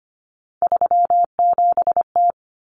Morse code, heard as a keyed tone, about 25 wpm, 700 Hz. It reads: 37T
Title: Common Cut Numbers - 3-Digit 25wpm